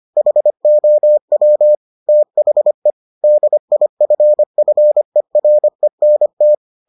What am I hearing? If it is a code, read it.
Morse code, heard as a keyed tone, about 25 wpm, 600 Hz. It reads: HOW THE DIFFERENT